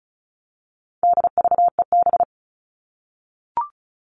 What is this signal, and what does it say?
Morse code, heard as a keyed tone, about 34 wpm, 700 Hz. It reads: DVEB
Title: All Cut Numbers - 4-Digit 35wpm